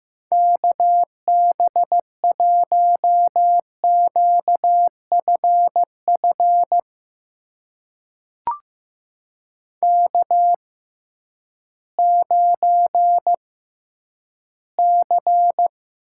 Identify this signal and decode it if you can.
Morse code, heard as a keyed tone, about 15 wpm, spaced as for 10 wpm, 700 Hz. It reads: KB1QFF K9C
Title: USA Call Signs - 6x Character Spacing - 15wpm